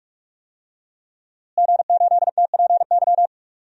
Morse code, heard as a keyed tone, about 46 wpm, 700 Hz. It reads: G8TPY
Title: DXCC-100 Call Signs - 5 Characters - Repeated 3x - 45wpm